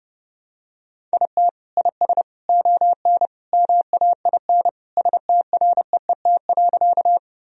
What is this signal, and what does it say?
Morse code, heard as a keyed tone, about 30 wpm, 700 Hz. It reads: IT IS ON MAIN STREET.
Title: Sentences from Top 1000 Words - Review Part 2 - 30wpm